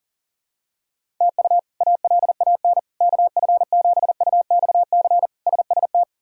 Morse code, heard as a keyed tone, about 40 wpm, 700 Hz. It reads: TU ALAN KF7UXC SST
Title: K1USN SST QSO Practice - 40wpm